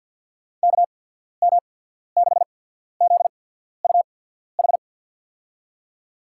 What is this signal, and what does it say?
Morse code, heard as a keyed tone, about 52 wpm, spaced as for 14 wpm, 700 Hz. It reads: KM6ZUH
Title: USA Call Signs - 6 Characters - 8x Character Spacing - 50wpm